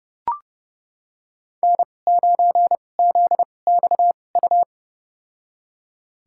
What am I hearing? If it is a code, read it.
Morse code, heard as a keyed tone, about 30 wpm, spaced as for 22 wpm, 700 Hz. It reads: N9ZXU
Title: USA Call Signs - 2x Character Spacing - 30wpm